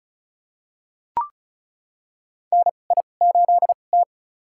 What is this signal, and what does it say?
Morse code, heard as a keyed tone, about 36 wpm, spaced as for 25 wpm, 700 Hz. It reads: NI8T
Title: USA Call Signs - 4 Characters - 2x Character Spacing - 35wpm